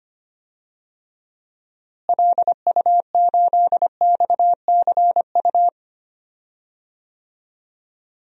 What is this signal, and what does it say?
Morse code, heard as a keyed tone, about 25 wpm, 700 Hz. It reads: LU8XCU
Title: DXCC-100 Call Signs - 6 Characters - Repeated 2x - 25wpm